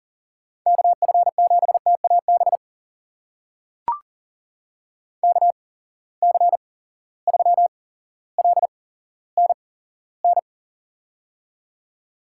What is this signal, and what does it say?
Morse code, heard as a keyed tone, about 39 wpm, spaced as for 17 wpm, 700 Hz. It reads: KF7TAB KC3LNN
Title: USA Call Signs - 6 Characters - 8x Character Spacing - 40wpm